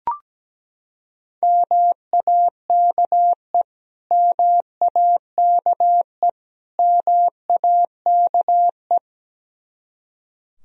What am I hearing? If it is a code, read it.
Morse code, heard as a keyed tone, about 17 wpm, 700 Hz. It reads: MAKE MAKE MAKE
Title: Top 100 Words Repeated 17wpm